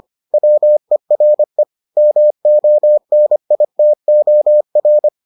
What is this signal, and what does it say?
Morse code, heard as a keyed tone, about 25 wpm, 600 Hz. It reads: WERE MONITOR